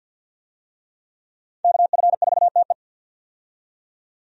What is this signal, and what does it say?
Morse code, heard as a keyed tone, about 53 wpm, 700 Hz. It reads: KF4TE
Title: USA Call Signs - 5 Characters - 50wpm